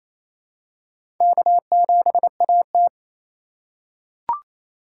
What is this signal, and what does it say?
Morse code, heard as a keyed tone, about 28 wpm, 700 Hz. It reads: K7AT
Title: USA Call Signs - 4 Characters - 8x Character Spacing - 28wpm